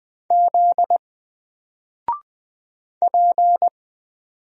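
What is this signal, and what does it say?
Morse code, heard as a keyed tone, about 20 wpm, 700 Hz. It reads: Z P
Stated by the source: Single letter and number Morse drill at 20 wpm, from N4LCD.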